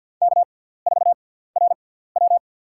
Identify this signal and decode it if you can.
Morse code, heard as a keyed tone, about 54 wpm, spaced as for 17 wpm, 690 Hz. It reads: K4RW
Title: USA Call Signs - 4 Characters - 6x Character Spacing - 50wpm